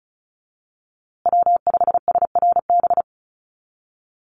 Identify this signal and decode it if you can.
Morse code, heard as a keyed tone, about 34 wpm, 700 Hz. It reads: W5SRB
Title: USA Call Signs Repeated 3x 35wpm